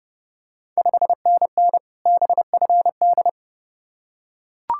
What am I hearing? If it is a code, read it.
Morse code, heard as a keyed tone, about 30 wpm, 700 Hz. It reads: 5NN BFD
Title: Ontario QSO Party Exchange 30wpm